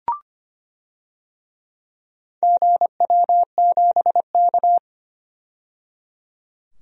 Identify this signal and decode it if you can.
Morse code, heard as a keyed tone, about 25 wpm, 700 Hz. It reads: GW7K
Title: DXCC-100 Call Signs - 4 Characters - 25wpm